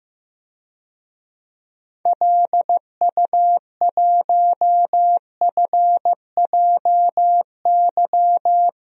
Morse code, heard as a keyed tone, about 16 wpm, 700 Hz. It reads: LU1FJY
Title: DXCC-100 Call Signs - 6 Characters - 15wpm